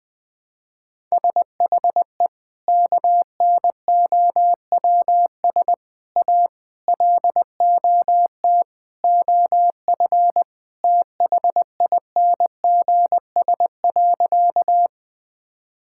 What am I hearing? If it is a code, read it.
Morse code, heard as a keyed tone, about 19 wpm, 700 Hz. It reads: SHE KNOWS A LOT OF THINGS.
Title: Sentences with 4-Word Phrases - Spoken Beginning - 20wpm